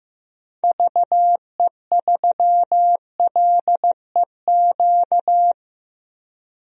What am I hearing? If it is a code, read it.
Morse code, heard as a keyed tone, about 15 wpm, 700 Hz. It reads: VE3LEQ
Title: DXCC-100 Call Signs - 6 Characters - Repeated 3x - 15wpm